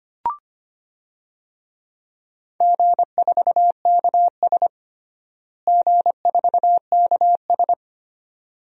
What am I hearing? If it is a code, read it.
Morse code, heard as a keyed tone, about 25 wpm, 700 Hz. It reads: G4KS G4KS
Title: DXCC-100 Call Signs - 4 Characters - Repeated 2x - 25wpm